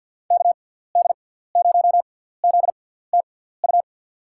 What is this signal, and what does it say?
Morse code, heard as a keyed tone, about 52 wpm, spaced as for 17 wpm, 700 Hz. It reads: KD0ZTU
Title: USA Call Signs - 6 Characters - 6x Character Spacing - 50wpm